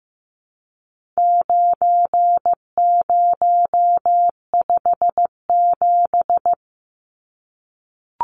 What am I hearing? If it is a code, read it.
Morse code, heard as a keyed tone, about 15 wpm, 700 Hz. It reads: 9057
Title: Numbers - 4-Digit 15wpm